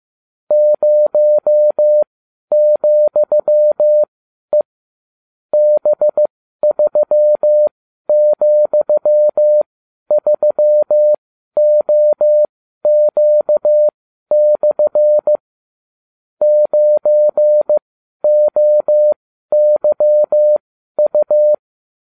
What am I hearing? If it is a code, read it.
Morse code, heard as a keyed tone, about 15 wpm, spaced as for 11 wpm, 600 Hz. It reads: 0,E B3,3OQ/ 9OYU